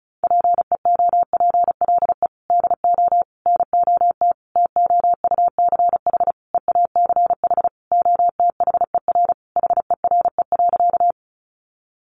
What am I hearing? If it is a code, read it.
Morse code, heard as a keyed tone, about 34 wpm, 700 Hz. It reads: PEOPLE DO NOT TOUCH EACH OTHER HERE.
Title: Sentences from Top 1200 Words 35wpm